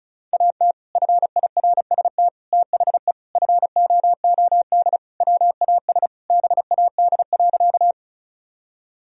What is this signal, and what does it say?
Morse code, heard as a keyed tone, about 36 wpm, 700 Hz. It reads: AT FIRST THE FOOD WAS BAD.
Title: Sentences from Top 1000 Words - Review Part 1 - 35wpm